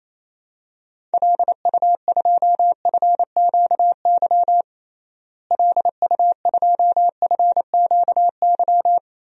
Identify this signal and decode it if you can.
Morse code, heard as a keyed tone, about 27 wpm, 700 Hz. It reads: LU2FQY LU2FQY
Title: DXCC-100 Call Signs - 6 Characters - Repeated 2x - 28wpm